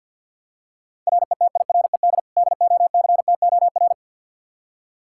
Morse code, heard as a keyed tone, about 51 wpm, 700 Hz. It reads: RETIRED DOCTOR
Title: Practice QSO - Exchange 3 - More Info - 50wpm